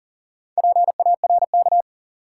Morse code, heard as a keyed tone, about 38 wpm, 700 Hz. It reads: PARK
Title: Sentences from Top 1300 Words - New Word Drill - 40wpm